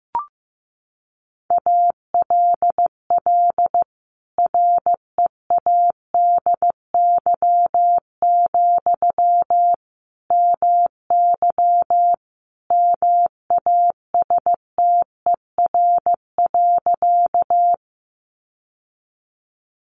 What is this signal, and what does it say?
Morse code, heard as a keyed tone, about 14 wpm, 700 Hz. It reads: ALL READY, MY MASTER.
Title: Sentences from Top 1300 Words 15wpm